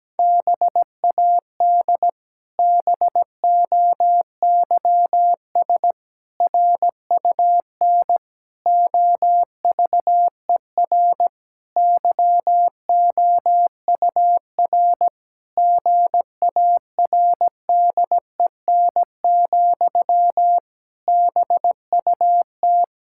Morse code, heard as a keyed tone, about 17 wpm, 700 Hz. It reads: BAD BOYS RUN OVER YOUR GARDEN, BUT
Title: Pangram Challenge v2 - 17wpm